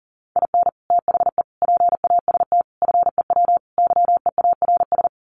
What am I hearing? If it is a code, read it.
Morse code, heard as a keyed tone, about 40 wpm, 700 Hz. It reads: IN THE PAST FEW YEARS
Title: Sets of 5 Words 40wpm